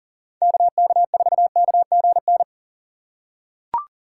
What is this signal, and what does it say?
Morse code, heard as a keyed tone, about 40 wpm, 700 Hz. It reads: KK4KGN
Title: USA Call Signs - 6 Characters - 2x Character Spacing - 40wpm